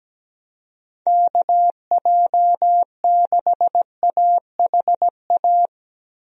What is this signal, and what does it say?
Morse code, heard as a keyed tone, about 17 wpm, 700 Hz. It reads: KJ6AHA
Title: USA Call Signs - 6 Characters - 17wpm